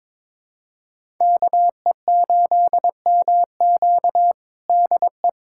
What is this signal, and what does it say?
Morse code, heard as a keyed tone, about 22 wpm, 700 Hz. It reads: KE8MQ DE
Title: USA Callsign DE Callsign 22wpm